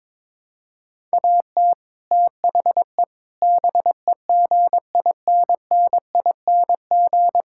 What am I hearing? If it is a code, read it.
Morse code, heard as a keyed tone, about 22 wpm, 700 Hz. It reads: AT THE BEGINNING
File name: practice / 22wpm / Build A Phrase - 5-Word Phrases - 22wpm